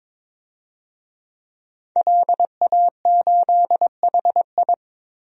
Morse code, heard as a keyed tone, about 21 wpm, 700 Hz. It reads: LA8HI
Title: DXCC-100 Call Signs - 5 Characters - 22wpm